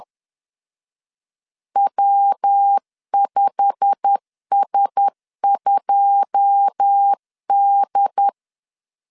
Morse code, heard as a keyed tone, about 10 wpm, 800 Hz. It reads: W5S2D
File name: CW / DET / 1 9 1500C S A 10wpm 4511